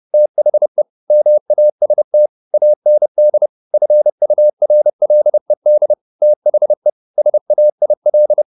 Morse code, heard as a keyed tone, about 30 wpm, 600 Hz. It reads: THE MAST AND FURLED THE SAIL